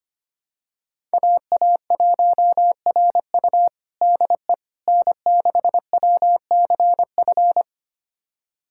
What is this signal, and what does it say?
Morse code, heard as a keyed tone, about 25 wpm, 700 Hz. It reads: AA1RU DE N6WCF
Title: USA Callsign DE Callsign - 5 Characters - 25wpm